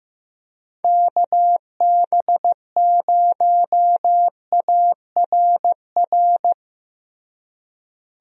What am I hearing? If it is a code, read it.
Morse code, heard as a keyed tone, about 15 wpm, 700 Hz. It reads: KB0ARR
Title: USA Call Signs - 6 Characters - Repeated 3x - 15wpm